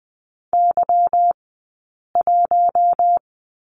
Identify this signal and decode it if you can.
Morse code, heard as a keyed tone, about 20 wpm, 700 Hz. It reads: Y 1
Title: 190 Lesson 31d - Letter v - Two Characters 20wpm